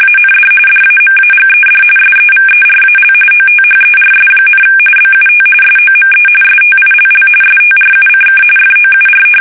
RTTY, about 45.45 baud. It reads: AND DAMAGED SEVERAL OTHERS, BUT SOME STILL STRUCK FOUR
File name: fsk850 / battleshipnj